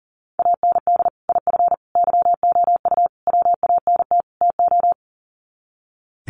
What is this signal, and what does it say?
Morse code, heard as a keyed tone, about 37 wpm, 700 Hz. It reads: AND IF YOU WANT TO
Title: Sets of 5 Words Encore 40wpm